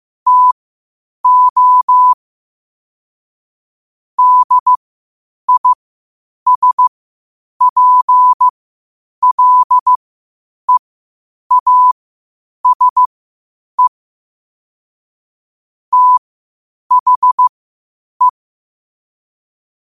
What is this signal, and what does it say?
Morse code, heard as a keyed tone, about 15 wpm, spaced as for 8 wpm, 1000 Hz. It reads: TO DISPLEASE THE
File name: qotdfiles / 8wpm